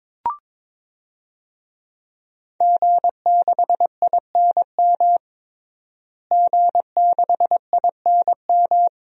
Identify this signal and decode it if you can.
Morse code, heard as a keyed tone, about 22 wpm, 700 Hz. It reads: G6INM G6INM
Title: DXCC-100 Call Signs - 5 Characters - Repeated 2x - 22wpm